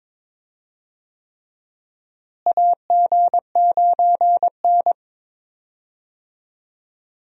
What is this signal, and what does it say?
Morse code, heard as a keyed tone, about 22 wpm, 700 Hz. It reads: AG9N
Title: USA Call Signs - 4 Characters - Repeated 3x - 22wpm